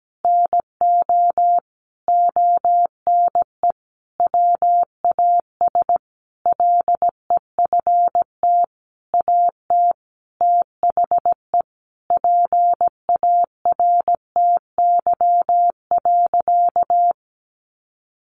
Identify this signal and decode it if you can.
Morse code, heard as a keyed tone, about 17 wpm, 700 Hz. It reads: NO ONE WAS LEFT AT THE PARTY.
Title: Sentences from Top 1400 Words 17wpm